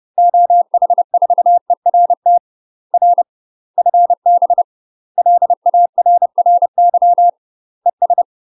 Morse code, heard as a keyed tone, about 30 wpm, 700 Hz. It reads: OH4ERT R FB LARRY ES
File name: UK MASTER QSOs TRAINING / 30WPM / UK MASTER QSOs 0008